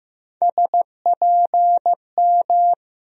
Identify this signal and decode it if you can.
Morse code, heard as a keyed tone, about 15 wpm, 700 Hz. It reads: SPM
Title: USA Call Sign Postfixes - 3 Letters 15wpm